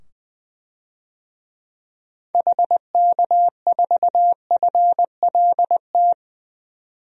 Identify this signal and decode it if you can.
Morse code, heard as a keyed tone, about 20 wpm, 700 Hz. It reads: HK4FLT
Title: DXCC-100 Call Signs - 6 Characters - 20wpm